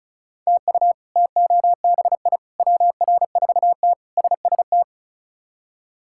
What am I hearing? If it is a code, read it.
Morse code, heard as a keyed tone, about 36 wpm, 700 Hz. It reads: TU TOBI WR4T SST
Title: K1USN SST QSO Practice - 35wpm